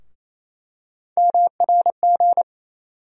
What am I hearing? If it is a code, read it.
Morse code, heard as a keyed tone, about 28 wpm, 700 Hz. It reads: MRG
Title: USA Call Sign Postfixes - 3 Letters 28wpm